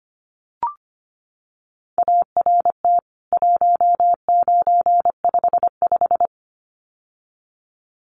Morse code, heard as a keyed tone, about 25 wpm, 700 Hz. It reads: ART 1955
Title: CWT Exchange 25wpm